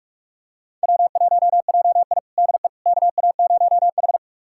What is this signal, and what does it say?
Morse code, heard as a keyed tone, about 46 wpm, 700 Hz. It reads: W1JI DE KA0H
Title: USA Callsign DE Callsign - 4 Characters - 45wpm